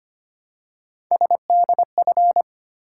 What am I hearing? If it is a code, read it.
Morse code, heard as a keyed tone, about 25 wpm, 700 Hz. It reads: SDF
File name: Three Letter-Number Combinations - 25wpm